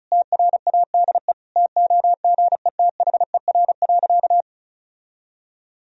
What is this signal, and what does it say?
Morse code, heard as a keyed tone, about 36 wpm, 700 Hz. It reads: TRADE TOGETHER.
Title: Sentences from Top 1000 Words - Review Part 4 - 35wpm